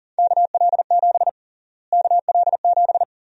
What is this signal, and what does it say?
Morse code, heard as a keyed tone, about 40 wpm, 700 Hz. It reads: KL7 KL7
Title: USA Call Sign Prefixes - Repeated 2x - 40wpm